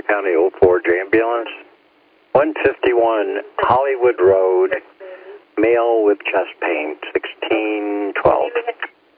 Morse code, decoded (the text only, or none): none